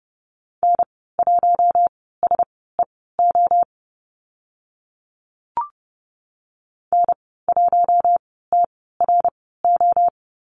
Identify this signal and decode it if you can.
Morse code, heard as a keyed tone, about 30 wpm, spaced as for 17 wpm, 700 Hz. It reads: N1SEO N1TRO
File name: Lesson 14g - Number 2 - Call Signs 30wpm